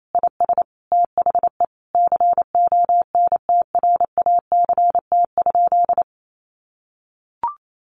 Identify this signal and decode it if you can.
Morse code, heard as a keyed tone, about 28 wpm, 700 Hz. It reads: IS THE CONTRACT?
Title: Sentences from Top 1400 Words 28wpm